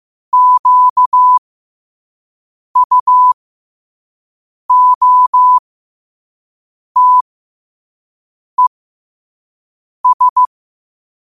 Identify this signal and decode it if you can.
Morse code, heard as a keyed tone, about 15 wpm, spaced as for 5 wpm, 1000 Hz. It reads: QUOTES
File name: qotdfiles / 5wpm